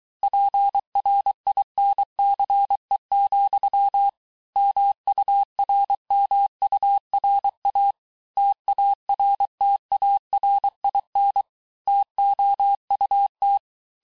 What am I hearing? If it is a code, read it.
Morse code, heard as a keyed tone, about 23 wpm, 800 Hz. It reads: PRINCE, MURMURA TARTARIN TOUT
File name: cours-du-mercredi / audios / COURS-855--texte-clair-vitesse-22-mpm